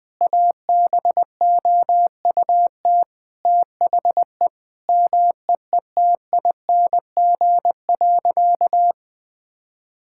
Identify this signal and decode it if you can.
Morse code, heard as a keyed tone, about 20 wpm, 700 Hz. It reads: ABOUT THE MEETING.
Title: Sentences with 5-Word Phrases - 20wpm